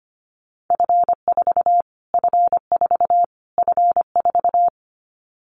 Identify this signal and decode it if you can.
Morse code, heard as a keyed tone, about 25 wpm, 700 Hz. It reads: F4 F4 F4
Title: Callsign Prefixes for Top 25 DXCC - Repeated 3x - 25wpm